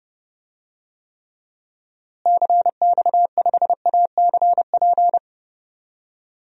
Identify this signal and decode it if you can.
Morse code, heard as a keyed tone, about 30 wpm, 700 Hz. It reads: CX5ACP
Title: DXCC-100 Call Signs - 6 Characters - Repeated 2x - 30wpm